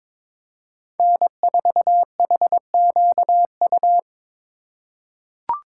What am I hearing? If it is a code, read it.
Morse code, heard as a keyed tone, about 23 wpm, 700 Hz. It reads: N4HQU
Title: USA Call Signs - 5 Characters - 3x Character Spacing - 22wpm